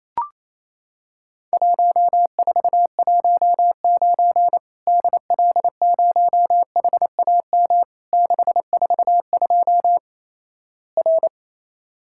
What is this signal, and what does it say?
Morse code, heard as a keyed tone, about 27 wpm, 700 Hz. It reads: 1419 DL0HAM 642 R
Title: Worked All Europe DX Contest - QTC Exchange - 28wpm